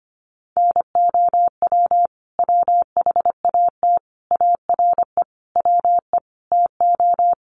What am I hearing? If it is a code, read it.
Morse code, heard as a keyed tone, about 25 wpm, 700 Hz. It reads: NOW WHAT ARE WE TO
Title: Sentences from Top 100 Words 25wpm